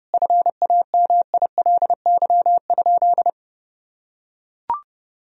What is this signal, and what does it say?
Morse code, heard as a keyed tone, about 30 wpm, 700 Hz. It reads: FAMILY?
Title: Sentences from Top 400 Words and Abbreviations 30wpm